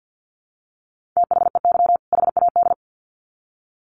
Morse code, heard as a keyed tone, about 50 wpm, 700 Hz. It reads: THEY HAD
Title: Sets of 2 Words 50wpm